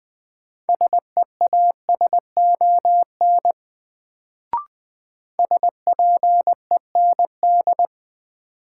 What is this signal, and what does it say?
Morse code, heard as a keyed tone, about 20 wpm, 700 Hz. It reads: SEASON SPEND
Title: Sets of 5 Words - New Word Drill - 20wpm